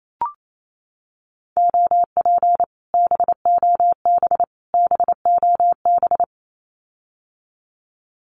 Morse code, heard as a keyed tone, about 28 wpm, 700 Hz. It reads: OP BOB BOB
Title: Common QSO Phrases Encore 28wpm